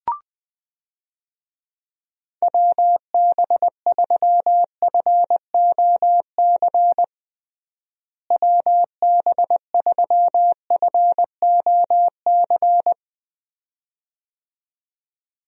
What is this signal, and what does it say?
Morse code, heard as a keyed tone, about 20 wpm, 700 Hz. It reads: WB3FOC WB3FOC
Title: USA Call Signs - 6 Characters - Repeated 2x - 20wpm